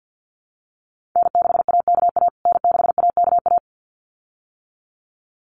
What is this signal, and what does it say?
Morse code, heard as a keyed tone, about 50 wpm, 700 Hz. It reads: N6AKA N6AKA
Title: USA Call Signs Repeated 2x 50wpm